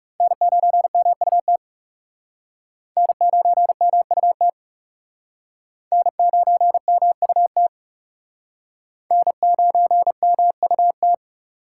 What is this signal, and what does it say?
Morse code, heard as a keyed tone, about 37 wpm, 700 Hz. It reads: N9MUT N9MUT N9MUT N9MUT
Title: Speed Racing - USA 1x3 Call Signs 30wpm